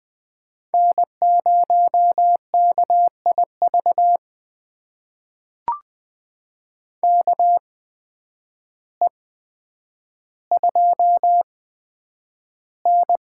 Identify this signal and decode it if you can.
Morse code, heard as a keyed tone, about 20 wpm, spaced as for 9 wpm, 700 Hz. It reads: N0KIV KE2N
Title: USA Call Signs - 8x Character Spacing - 20wpm